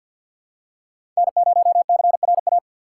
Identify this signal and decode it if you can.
Morse code, heard as a keyed tone, about 49 wpm, 690 Hz. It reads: N0CRA
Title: USA Call Signs - 5 Characters - 3x Character Spacing - 50wpm